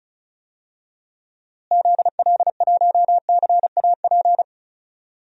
Morse code, heard as a keyed tone, about 34 wpm, 700 Hz. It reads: ZL1CAP
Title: DXCC-100 Call Signs - 6 Characters - Repeated 2x - 35wpm